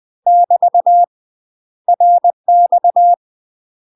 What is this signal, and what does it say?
Morse code, heard as a keyed tone, about 20 wpm, 700 Hz. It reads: <BT> RX
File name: UK MASTER QSOs TRAINING / 20WPM / UK MASTER QSOs 0037